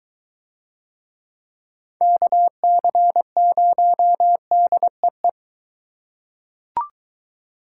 Morse code, heard as a keyed tone, about 21 wpm, 700 Hz. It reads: KC0DEE